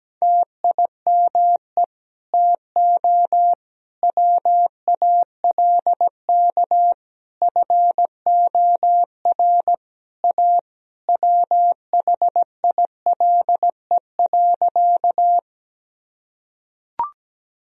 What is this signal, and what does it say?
Morse code, heard as a keyed tone, about 17 wpm, 700 Hz. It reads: TIME TO WALK FOR A WHILE.
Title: Sentences from Top 1000 Words - Review Part 1 - 17wpm